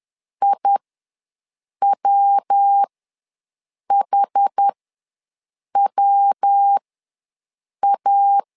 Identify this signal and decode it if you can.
Morse code, heard as a keyed tone, about 10 wpm, spaced as for 6 wpm, 800 Hz. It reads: IWHWA
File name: LEZIONI / 3 MWRO / 3 MW ws 10wpm 2213